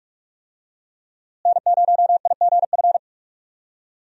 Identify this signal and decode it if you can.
Morse code, heard as a keyed tone, about 42 wpm, 700 Hz. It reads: N0IGF
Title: USA Call Signs - 5 Characters - Repeated 3x - 45wpm